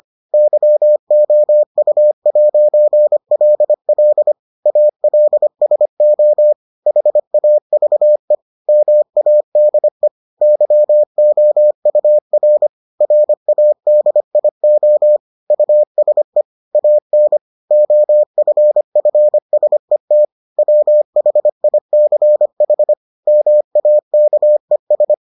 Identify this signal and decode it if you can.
Morse code, heard as a keyed tone, about 25 wpm, 600 Hz. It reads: YOU'LL ALSO HAVE MADE YOUR RADIO USE AN OFFSET WHICH MAKES